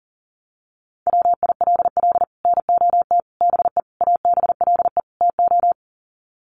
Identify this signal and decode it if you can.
Morse code, heard as a keyed tone, about 40 wpm, 700 Hz. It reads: WILL NOT BE ABLE TO
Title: Sets of 5 Words 40wpm